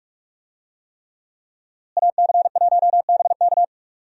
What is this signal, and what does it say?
Morse code, heard as a keyed tone, about 46 wpm, 700 Hz. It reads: AC1BK
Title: USA Call Signs - 5 Characters - Repeated 3x - 45wpm